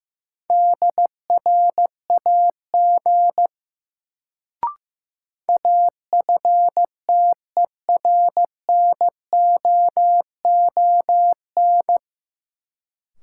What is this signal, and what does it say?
Morse code, heard as a keyed tone, about 16 wpm, 700 Hz. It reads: DRAG AFTERNOON
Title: Sentences from Top 1500 Words - New Word Drill Alt - 15wpm